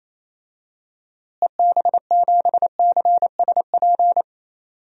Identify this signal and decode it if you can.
Morse code, heard as a keyed tone, about 28 wpm, 700 Hz. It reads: EB7CSP
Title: DXCC-100 Call Signs - 6 Characters - Repeated 3x - 28wpm